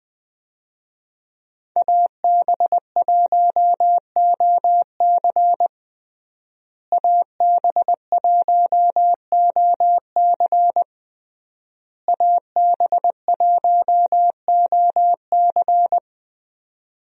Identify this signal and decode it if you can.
Morse code, heard as a keyed tone, about 20 wpm, 700 Hz. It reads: AB1OC AB1OC AB1OC
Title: USA Call Signs - 5 Characters - Repeated 3x - 20wpm